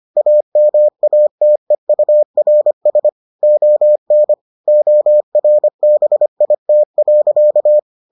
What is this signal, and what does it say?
Morse code, heard as a keyed tone, about 25 wpm, 600 Hz. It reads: AMATEURS ON ORBIT.